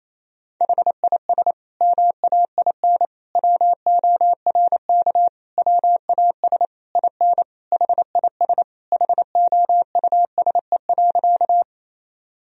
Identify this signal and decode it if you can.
Morse code, heard as a keyed tone, about 27 wpm, 700 Hz. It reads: HIS MAIN WORK WAS IN HIS HOUSE.
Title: Sentences from Top 1000 Words - Review Part 2 - 28wpm